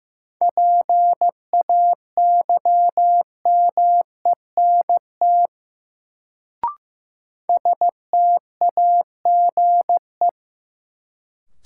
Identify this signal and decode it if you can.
Morse code, heard as a keyed tone, about 15 wpm, 700 Hz. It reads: PAYMENT STAGE
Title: Sentences from Top 1200 Words - New Word Drill Alt - 15wpm